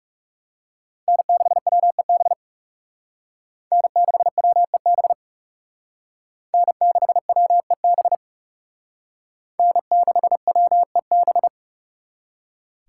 Morse code, heard as a keyed tone, about 38 wpm, 700 Hz. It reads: N6WEB N6WEB N6WEB N6WEB
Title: Speed Racing - USA Call Signs 30wpm